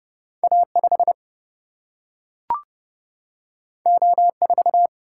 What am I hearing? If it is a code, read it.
Morse code, heard as a keyed tone, about 30 wpm, 700 Hz. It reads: A5 O4
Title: Single-Letter Single-Number 30wpm